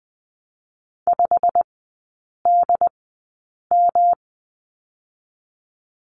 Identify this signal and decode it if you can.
Morse code, heard as a keyed tone, about 20 wpm, 700 Hz. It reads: 5 D M
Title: 172 Lesson 28e - Number 9 - Three Characters 20wpm